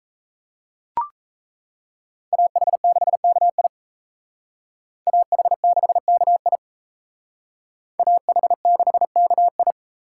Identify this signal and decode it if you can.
Morse code, heard as a keyed tone, about 38 wpm, 700 Hz. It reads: AH6KI AH6KI AH6KI